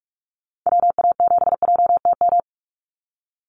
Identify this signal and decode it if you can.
Morse code, heard as a keyed tone, about 46 wpm, 700 Hz. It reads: WA7JTM
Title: NA SOTA Chaser Callsigns 45wpm